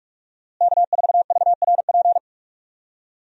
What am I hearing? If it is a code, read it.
Morse code, heard as a keyed tone, about 46 wpm, 700 Hz. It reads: K4VRP
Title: USA Call Signs - 5 Characters - Repeated 2x - 45wpm